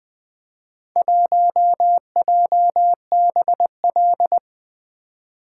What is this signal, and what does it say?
Morse code, heard as a keyed tone, about 20 wpm, 700 Hz. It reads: 1JBL
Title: Four Letter-Number Combinations - 20wpm